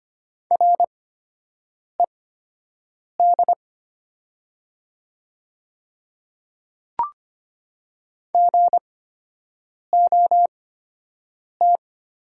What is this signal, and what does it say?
Morse code, heard as a keyed tone, about 25 wpm, spaced as for 7 wpm, 700 Hz. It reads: RED GOT
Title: 8x Character Spacing - Top 500 Words - 25wpm